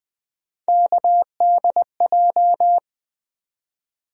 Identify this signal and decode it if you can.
Morse code, heard as a keyed tone, about 20 wpm, 700 Hz. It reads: KDJ
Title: USA Call Sign Postfixes - 3 Letters 20wpm